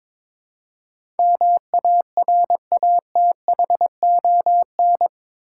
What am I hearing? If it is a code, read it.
Morse code, heard as a keyed tone, about 22 wpm, 700 Hz. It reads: MARATHON
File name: practice / 22wpm / Eight Letter Words - 22wpm